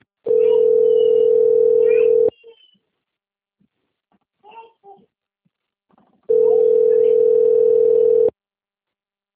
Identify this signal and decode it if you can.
Ringback tone (North America), 440 + 480 Hz.